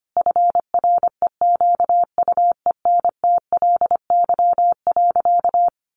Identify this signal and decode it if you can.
Morse code, heard as a keyed tone, about 25 wpm, 700 Hz. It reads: FREQUENTLY.
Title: Sentences from Top 1300 Words 25wpm